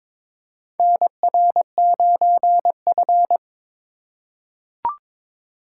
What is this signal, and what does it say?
Morse code, heard as a keyed tone, about 22 wpm, 700 Hz. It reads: NR9F
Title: USA Call Signs - 2x Character Spacing - 22wpm